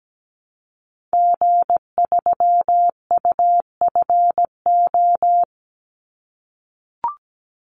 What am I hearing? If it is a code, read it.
Morse code, heard as a keyed tone, about 17 wpm, 700 Hz. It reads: G3UFO
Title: Call Signs - 17wpm